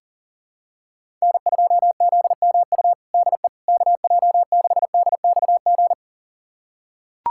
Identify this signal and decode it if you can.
Morse code, heard as a keyed tone, about 40 wpm, 700 Hz. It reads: N2ZMU DE KJ6DXG
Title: USA Callsign DE Callsign 40wpm